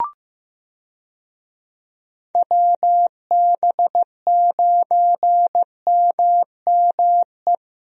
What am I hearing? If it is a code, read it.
Morse code, heard as a keyed tone, about 15 wpm, 700 Hz. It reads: WB9MME
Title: USA Call Signs - 6 Characters - Repeated 3x - 15wpm